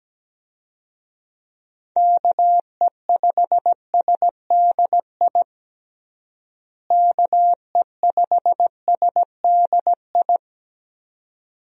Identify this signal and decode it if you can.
Morse code, heard as a keyed tone, about 17 wpm, 700 Hz. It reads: KE5SDI KE5SDI
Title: USA Call Signs - 6 Characters - Repeated 2x - 17wpm